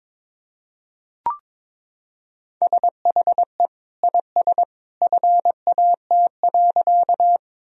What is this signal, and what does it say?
Morse code, heard as a keyed tone, about 22 wpm, 700 Hz. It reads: SHE IS FAT.
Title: Sentences from Top 1000 Words - Review Part 4 - 22wpm